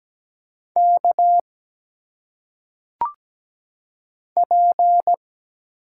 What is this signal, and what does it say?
Morse code, heard as a keyed tone, about 17 wpm, 700 Hz. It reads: K P
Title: Single Letters - Phonetic Pronunciation 17wpm